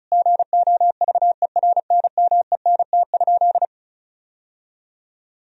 Morse code, heard as a keyed tone, about 36 wpm, 700 Hz. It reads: GOVERNMENT?
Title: Sentences from Top 1000 Words - Review Part 4 - 35wpm